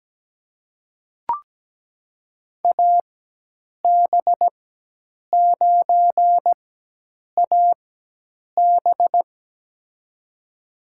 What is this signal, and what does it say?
Morse code, heard as a keyed tone, about 17 wpm, spaced as for 8 wpm, 700 Hz. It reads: AB9AB
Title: USA Call Signs - 5 Characters - 4x Character Spacing - 17wpm